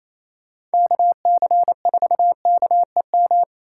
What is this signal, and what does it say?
Morse code, heard as a keyed tone, about 33 wpm, 700 Hz. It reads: KC4KEM
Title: Speed Racing - USA 2x3 Call Signs 28wpm